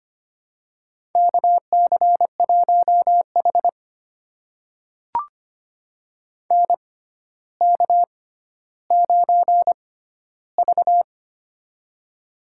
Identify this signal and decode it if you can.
Morse code, heard as a keyed tone, about 25 wpm, spaced as for 13 wpm, 700 Hz. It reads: KC1H NK9V
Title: USA Call Signs - 4 Characters - 6x Character Spacing - 25wpm